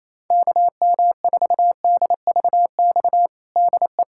Morse code, heard as a keyed tone, about 28 wpm, 700 Hz. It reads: KM4DVX DE
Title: USA Callsign DE Callsign 28wpm